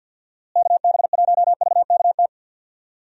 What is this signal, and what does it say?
Morse code, heard as a keyed tone, about 53 wpm, 700 Hz. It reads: KB1VKT
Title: USA Call Signs - 6 Characters - Repeated 3x - 50wpm